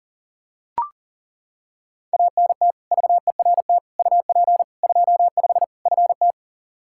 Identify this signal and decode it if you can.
Morse code, heard as a keyed tone, about 40 wpm, 700 Hz. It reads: ANT VERT UP 25 FT
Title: K1USN SST QSO Practice 40wpm